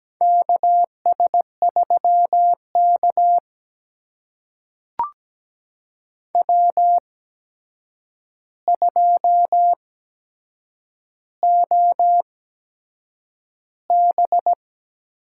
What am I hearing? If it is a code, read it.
Morse code, heard as a keyed tone, about 17 wpm, spaced as for 7 wpm, 700 Hz. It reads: KS3K W2OB
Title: USA Call Signs - 4 Characters - 8x Character Spacing - 17wpm